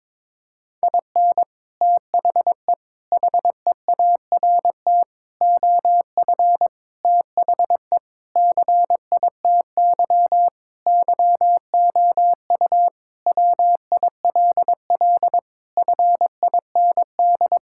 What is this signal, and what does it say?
Morse code, heard as a keyed tone, about 22 wpm, 700 Hz. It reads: IN THE HEART OF THE CITY YOU WILL FIND
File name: practice / 22wpm / Sentences with 3-Word Phrases - 22wpm